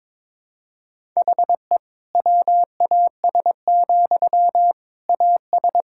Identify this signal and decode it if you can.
Morse code, heard as a keyed tone, about 22 wpm, 700 Hz. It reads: HE WAS, AS
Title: Sentences from Top 1000 Words - Review Part 6 - 22wpm